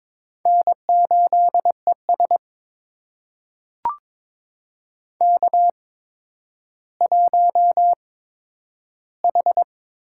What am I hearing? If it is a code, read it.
Morse code, heard as a keyed tone, about 22 wpm, spaced as for 11 wpm, 700 Hz. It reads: N8ES K1H
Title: USA Call Signs - 4 Characters - 8x Character Spacing - 22wpm